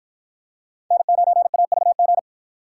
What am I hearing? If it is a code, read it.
Morse code, heard as a keyed tone, about 53 wpm, 700 Hz. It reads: N9AVG